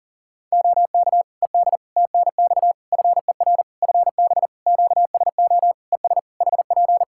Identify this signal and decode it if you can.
Morse code, heard as a keyed tone, about 37 wpm, 700 Hz. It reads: OK ED TNX FER FB QSO ES HP
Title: Practice QSO - Exchange 4 - Ending - 40wpm